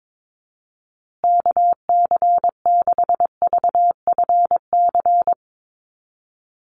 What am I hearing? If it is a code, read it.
Morse code, heard as a keyed tone, about 22 wpm, 700 Hz. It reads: KC6VFC
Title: USA Call Signs Repeated 2x 22wpm